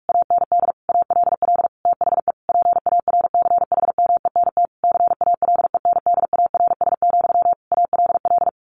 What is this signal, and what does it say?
Morse code, heard as a keyed tone, about 44 wpm, 700 Hz. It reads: AND ALL THE PARCHMENT CALENDARS, ALL
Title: A Sound Of Thunder 45wpm